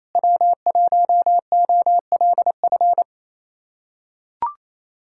Morse code, heard as a keyed tone, about 27 wpm, 700 Hz. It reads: W1OLF
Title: USA Callsign DE Callsign - 5 Characters - 28wpm